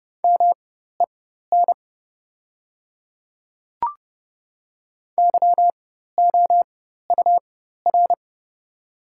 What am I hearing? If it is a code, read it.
Morse code, heard as a keyed tone, about 30 wpm, spaced as for 14 wpm, 700 Hz. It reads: MEN YOUR
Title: 4x Character Spacing - Top 300 Words - 30wpm